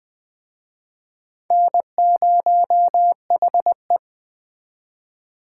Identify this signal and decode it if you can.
Morse code, heard as a keyed tone, about 21 wpm, 700 Hz. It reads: N0HE